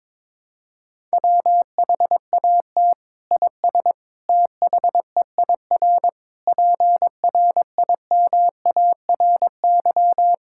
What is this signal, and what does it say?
Morse code, heard as a keyed tone, about 22 wpm, 700 Hz. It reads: WHAT IS THEIR PRIMARY
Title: Sentences from Top 1000 Words - Review Part 5 - 22wpm